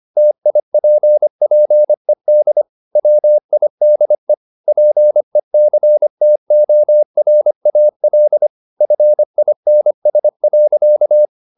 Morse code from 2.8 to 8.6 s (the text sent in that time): WIDE PECTORAL